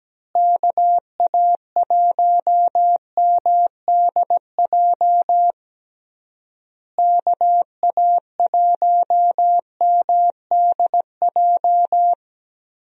Morse code, heard as a keyed tone, about 17 wpm, 700 Hz. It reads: KA1MDJ KA1MDJ
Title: USA Call Signs - 6 Characters - Repeated 3x - 17wpm